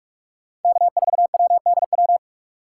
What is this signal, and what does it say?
Morse code, heard as a keyed tone, about 46 wpm, 700 Hz. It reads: K4WDW
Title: USA Call Signs - 5 Characters - 2x Character Spacing - 45wpm